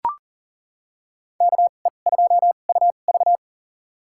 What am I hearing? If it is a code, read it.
Morse code, heard as a keyed tone, about 40 wpm, spaced as for 29 wpm, 700 Hz. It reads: KE2UV
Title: USA Call Signs - 5 Characters - 2x Character Spacing - 40wpm